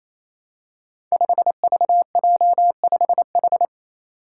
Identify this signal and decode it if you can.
Morse code, heard as a keyed tone, about 27 wpm, 700 Hz. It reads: 5VJ5H
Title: Five Challenging Letter-Number Combinations 28wpm